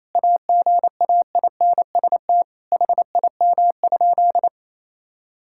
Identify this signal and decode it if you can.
Morse code, heard as a keyed tone, about 28 wpm, 700 Hz. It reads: AGAINST HIM?
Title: Sentences from Top 1000 Words - Review Part 6 - 28wpm